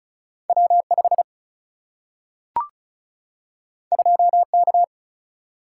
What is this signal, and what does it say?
Morse code, heard as a keyed tone, about 36 wpm, 700 Hz. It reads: W 5 2 K
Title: Single-Letter Single-Number 35wpm